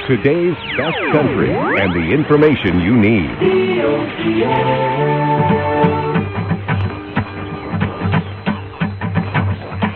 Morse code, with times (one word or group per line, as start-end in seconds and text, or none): none